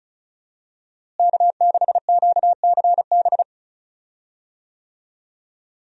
Morse code, heard as a keyed tone, about 34 wpm, 700 Hz. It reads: K6QCB
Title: USA Call Signs - 5 Characters - Repeated 2x - 35wpm